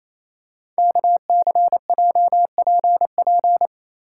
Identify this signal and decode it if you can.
Morse code, heard as a keyed tone, about 28 wpm, 700 Hz. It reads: KCJPP
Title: Code Groups - 5 Letters - 28wpm